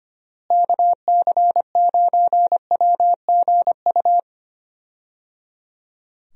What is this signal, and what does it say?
Morse code, heard as a keyed tone, about 25 wpm, 700 Hz. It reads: KC9WGU
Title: USA Call Signs - 6 Characters - Repeated 3x - 25wpm